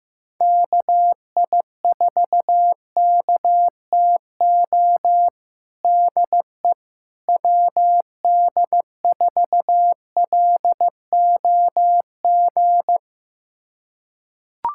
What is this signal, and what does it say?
Morse code, heard as a keyed tone, about 15 wpm, 700 Hz. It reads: KI4KTO DE WD4LOG
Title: USA Callsign DE Callsign - 6 Characters - 15wpm